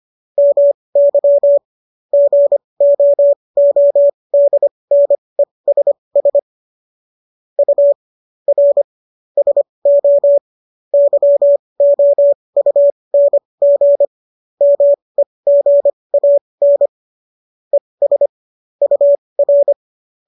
Morse code, heard as a keyed tone, about 25 wpm, spaced as for 20 wpm, 570 Hz. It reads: MY GOODNESS U R SO YOUNG MEGAN ES UR